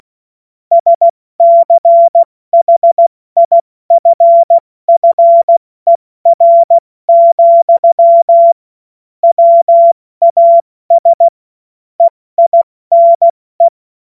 Morse code, heard as a keyed tone, about 16 wpm, 680 Hz. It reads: SCHIFFER, WAS EINE
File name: Morsekurs / Witze / Witz 21. Februar 2023 70 F80